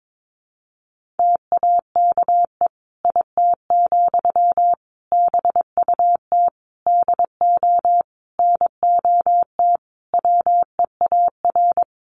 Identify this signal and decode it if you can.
Morse code, heard as a keyed tone, about 22 wpm, 700 Hz. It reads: TAKE IT, BUT DO NOT WEAR
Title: Sentences from Top 1500 Words 22wpm